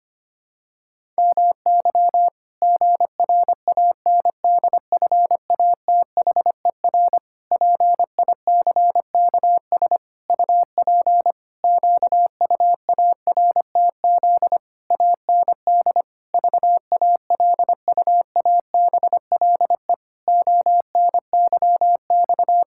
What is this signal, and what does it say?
Morse code, heard as a keyed tone, about 25 wpm, 700 Hz. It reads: MY GRANDFATHER PICKS UP QUARTZ AND VALUABLE ONYX
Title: Pangram Challenge v2 - 25wpm